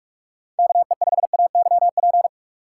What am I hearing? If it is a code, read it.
Morse code, heard as a keyed tone, about 45 wpm, 700 Hz. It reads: KE5AYP
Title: USA Call Signs - 6x Character Spacing - 45wpm